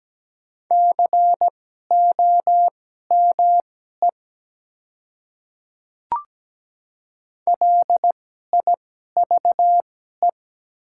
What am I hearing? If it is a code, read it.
Morse code, heard as a keyed tone, about 17 wpm, spaced as for 12 wpm, 700 Hz. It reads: COME LIVE
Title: 2x Character Spacing - Top 200 Words - 17wpm